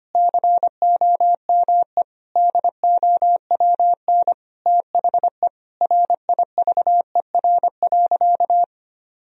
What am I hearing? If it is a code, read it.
Morse code, heard as a keyed tone, about 25 wpm, 700 Hz. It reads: COME DOWN THE RIVER.
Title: Sentences from Top 1000 Words - Review Part 2 - 25wpm